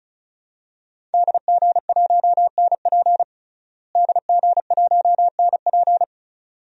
Morse code, heard as a keyed tone, about 34 wpm, 700 Hz. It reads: DG1NP DG1NP
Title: DXCC-100 Call Signs - 5 Characters - Repeated 2x - 35wpm